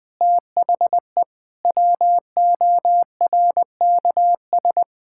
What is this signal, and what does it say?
Morse code, heard as a keyed tone, about 21 wpm, 700 Hz. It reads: THE WORKS
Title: Sentences with 6-Word Phrases - Spoken Beginning - 20wpm